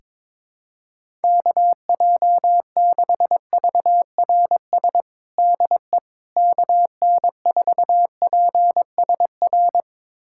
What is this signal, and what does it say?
Morse code, heard as a keyed tone, about 22 wpm, 700 Hz. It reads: KJ6VRS DE KN4PSR
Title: USA Callsign DE Callsign - 6 Characters - 22wpm